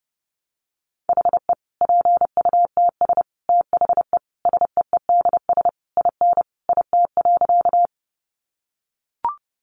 Morse code, heard as a keyed tone, about 30 wpm, 700 Hz. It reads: HE PUTS THE SEEDS IN IT.
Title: Easy Phrases 30wpm